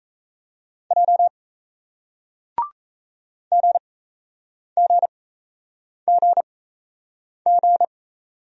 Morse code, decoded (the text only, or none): J GGGG